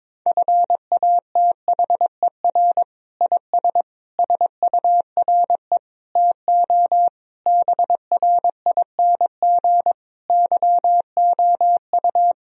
Morse code, heard as a keyed tone, about 22 wpm, 700 Hz. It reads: FATHER IS SURE TO BRING YOU
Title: Sentences from Top 1000 Words - Review Part 3 - 22wpm